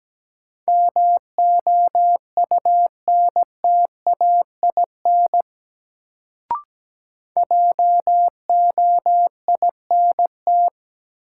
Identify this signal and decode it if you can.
Morse code, heard as a keyed tone, about 17 wpm, 700 Hz. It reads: MOUNTAIN JOINT
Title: Sentences from Top 1300 Words - New Word Drill Alt - 17wpm